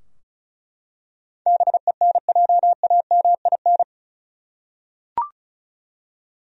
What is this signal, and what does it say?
Morse code, heard as a keyed tone, about 36 wpm, 700 Hz. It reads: BENJAMIN
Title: USA Common Names - 35wpm